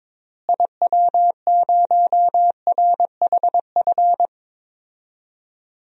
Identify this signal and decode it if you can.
Morse code, heard as a keyed tone, about 22 wpm, 700 Hz. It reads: IW0RHF
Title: DXCC-100 Call Signs - 6 Characters - Repeated 3x - 22wpm